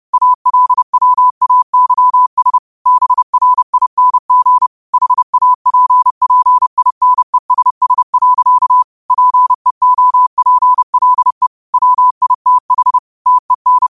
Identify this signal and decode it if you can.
Morse code, heard as a keyed tone, about 30 wpm, 1000 Hz. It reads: ALWAYS BRING HAPPINESS. PEOPLE WITH TEN
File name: qotdfiles / 30wpm